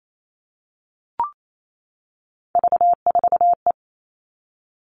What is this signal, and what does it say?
Morse code, heard as a keyed tone, about 28 wpm, 700 Hz. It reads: V4E
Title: Three Challenging Letter-Number Combinations 28wpm